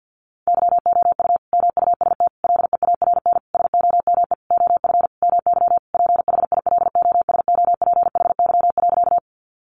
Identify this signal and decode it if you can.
Morse code, heard as a keyed tone, about 50 wpm, 700 Hz. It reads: YOU MUST LEARN SOME OF MY PHILOSOPHY.
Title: Sentences from Top 900 Words 50wpm